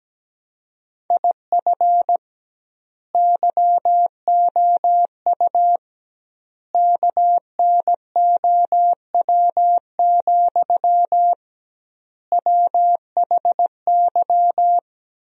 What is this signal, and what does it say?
Morse code, heard as a keyed tone, about 17 wpm, 700 Hz. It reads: IF YOU KNOW, WHY
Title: 2x Word-Spacing - Sentences from Top 300 Words - 17wpm